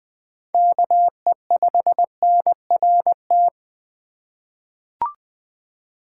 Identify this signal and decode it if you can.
Morse code, heard as a keyed tone, about 20 wpm, 700 Hz. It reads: KE5NRT
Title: USA Call Signs - 6 Characters - 4x Character Spacing - 20wpm